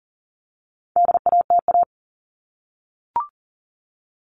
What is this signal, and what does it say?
Morse code, heard as a keyed tone, about 40 wpm, 700 Hz. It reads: DATA
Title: Top 501-750 Words 40wpm